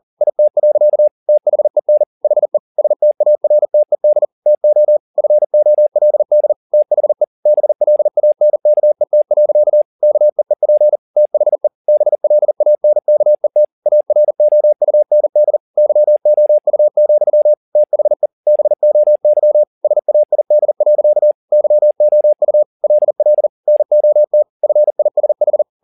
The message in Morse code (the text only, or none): IT. THEN HE STARTED TO FOLD THE BLANKET. KEEP THE BLANKET AROUND YOU, THE BOY SAID. YOU LL NOT FISH